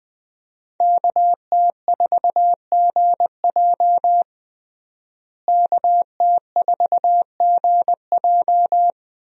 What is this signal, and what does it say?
Morse code, heard as a keyed tone, about 20 wpm, 700 Hz. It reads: KT4GJ KT4GJ
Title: USA Call Signs - 5 Characters - Repeated 2x - 20wpm